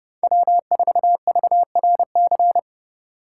Morse code, heard as a keyed tone, about 29 wpm, 700 Hz. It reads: W4VRC